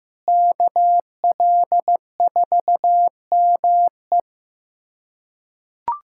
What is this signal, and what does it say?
Morse code, heard as a keyed tone, about 15 wpm, 700 Hz. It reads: KL4ME
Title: USA Callsign DE Callsign 15wpm